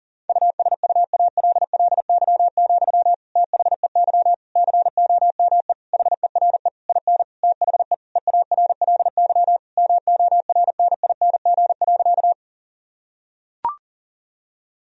Morse code, heard as a keyed tone, about 40 wpm, 700 Hz. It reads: USUALLY, THEY COME HERE IN THE EARLY MORNING.
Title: Sentences from Top 1000 Words - Review Part 3 - 40wpm